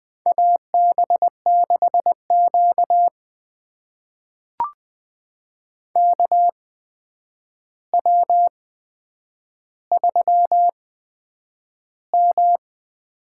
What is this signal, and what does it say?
Morse code, heard as a keyed tone, about 20 wpm, spaced as for 9 wpm, 700 Hz. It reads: AB6Q KW3M
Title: USA Call Signs - 4 Characters - 8x Character Spacing - 20wpm